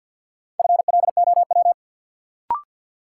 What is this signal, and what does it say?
Morse code, heard as a keyed tone, about 53 wpm, 700 Hz. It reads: FLOW
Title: Sentences from Top 1200 Words - New Word Drill Alt - 50wpm